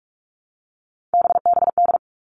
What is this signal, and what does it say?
Morse code, heard as a keyed tone, about 44 wpm, 700 Hz. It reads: BBD
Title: All Cut Numbers - 3-Digit 45wpm